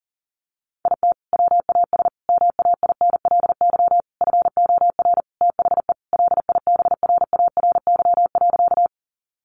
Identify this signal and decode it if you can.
Morse code, heard as a keyed tone, about 40 wpm, 700 Hz. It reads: IT WAS MAINLY FOR THE LIBRARY.
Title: Sentences from Top 1400 Words 40wpm